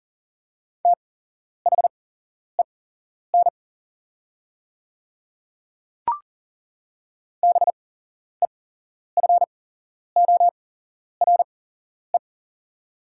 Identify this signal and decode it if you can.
Morse code, heard as a keyed tone, about 40 wpm, spaced as for 11 wpm, 690 Hz. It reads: THEN BEFORE
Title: 8x Character Spacing - Top 300 Words - 40wpm